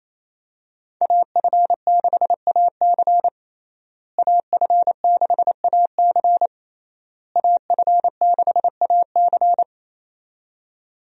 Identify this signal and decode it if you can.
Morse code, heard as a keyed tone, about 28 wpm, 700 Hz. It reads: AF6AC AF6AC AF6AC
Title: USA Call Signs - 5 Characters - Repeated 3x - 28wpm